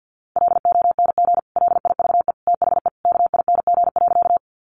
Morse code, heard as a keyed tone, about 50 wpm, 700 Hz. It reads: LONG LIVE THE KING.
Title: Sentences from Top 400 Words 50wpm